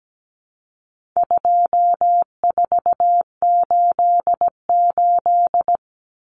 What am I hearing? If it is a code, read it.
Morse code, heard as a keyed tone, about 17 wpm, 700 Hz. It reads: 2488
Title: Numbers - 4-Digit 17wpm